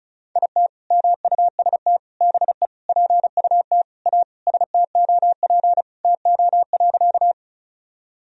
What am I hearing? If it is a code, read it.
Morse code, heard as a keyed tone, about 34 wpm, 700 Hz. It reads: IT MUST BE PUT A STOP TO.
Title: Sentences from Top 1000 Words - Review Part 1 - 35wpm